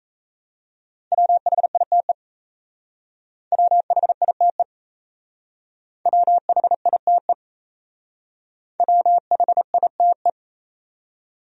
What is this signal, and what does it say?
Morse code, heard as a keyed tone, about 34 wpm, 700 Hz. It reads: WHITE WHITE WHITE WHITE
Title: Speed-Racing Top 500 Words 28wpm